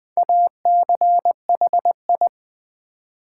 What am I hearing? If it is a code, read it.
Morse code, heard as a keyed tone, about 20 wpm, 700 Hz. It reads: ACHI
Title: Build A Word - 4 Letters At A Time - 11-Letter Words - 20wpm